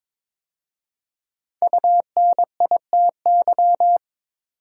Morse code, heard as a keyed tone, about 22 wpm, 700 Hz. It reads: UNITY
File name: 5 Letter Words - First 3 Letters Given - 22wpm